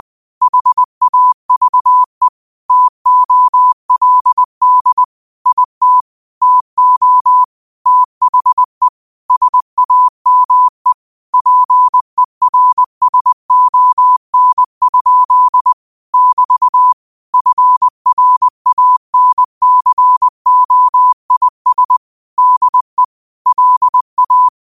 Morse code, heard as a keyed tone, about 20 wpm, 1000 Hz. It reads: HAVE TOLD IT TO THE SAME PERSON? <BT> FRANCOIS DE LA
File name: qotdfiles / 20wpm